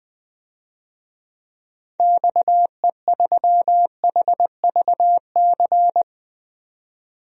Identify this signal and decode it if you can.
Morse code, heard as a keyed tone, about 20 wpm, 700 Hz. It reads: XE3HVC
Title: DXCC-100 Call Signs - 6 Characters - 20wpm